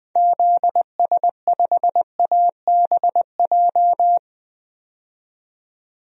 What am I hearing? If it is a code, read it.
Morse code, heard as a keyed tone, about 20 wpm, 700 Hz. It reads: ZS5ABJ
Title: DXCC-100 Call Signs - 6 Characters - 20wpm